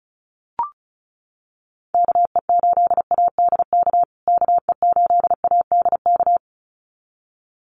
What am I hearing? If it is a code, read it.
Morse code, heard as a keyed tone, about 34 wpm, 700 Hz. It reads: KE8ADK KE8ADK
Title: USA Call Signs Repeated 2x 35wpm